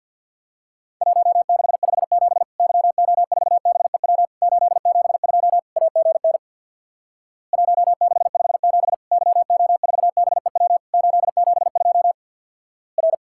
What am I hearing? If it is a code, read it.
Morse code, heard as a keyed tone, about 50 wpm, 700 Hz. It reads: 1657 YO4BEW 872 AGN 1657 YO4BEW 872 R
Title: Worked All Europe DX Contest - QTC Exchange with Repeats - 50wpm